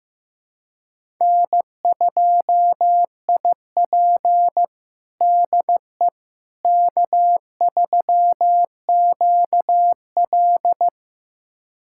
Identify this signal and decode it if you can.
Morse code, heard as a keyed tone, about 16 wpm, 700 Hz. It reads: N2IP DE K3QL
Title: USA Callsign DE Callsign - 4 Characters - 15wpm